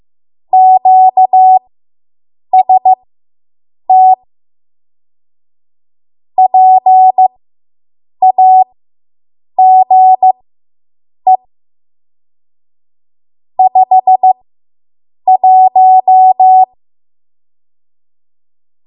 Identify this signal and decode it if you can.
Morse code, heard as a keyed tone, about 15 wpm, spaced as for 7 wpm, 750 Hz. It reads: QST PAGE 51